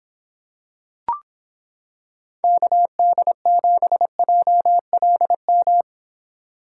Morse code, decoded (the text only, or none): KD7JLM